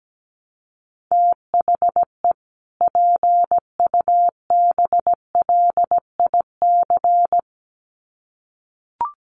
Morse code, heard as a keyed tone, about 17 wpm, 700 Hz. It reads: THE PUBLIC
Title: Sets of 2 Words 17wpm